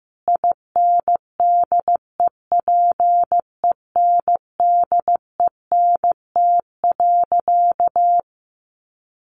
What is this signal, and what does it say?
Morse code, heard as a keyed tone, about 15 wpm, 700 Hz. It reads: INDEPENDENT.
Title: Sentences from Top 900 Words 15wpm